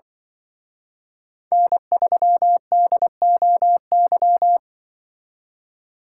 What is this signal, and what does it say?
Morse code, heard as a keyed tone, about 24 wpm, 700 Hz. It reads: N3DOY
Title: Speed Racing - USA Call Signs 20wpm